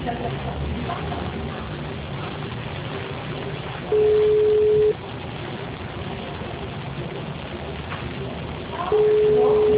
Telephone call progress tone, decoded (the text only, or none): ringback tone